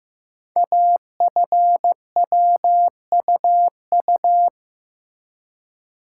Morse code, heard as a keyed tone, about 15 wpm, 700 Hz. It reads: AFWUU
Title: Code Groups - 5 Letters - 15wpm